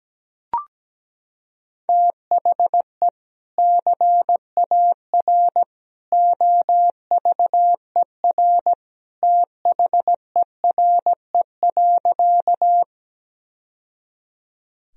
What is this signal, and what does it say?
Morse code, heard as a keyed tone, about 17 wpm, 700 Hz. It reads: THE CAR OVER THERE.
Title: Sentences from Top 1000 Words - Review Part 1 - 17wpm